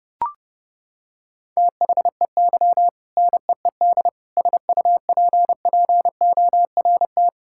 Morse code, heard as a keyed tone, about 30 wpm, 700 Hz. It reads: THEY NEED SUPPORT
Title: Sentences from Top 1000 Words - Review Part 5 - 30wpm